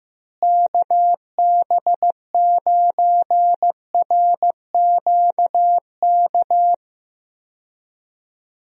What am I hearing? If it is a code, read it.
Morse code, heard as a keyed tone, about 15 wpm, 700 Hz. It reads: KB9RQK
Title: USA Call Signs - 6 Characters - Repeated 2x - 15wpm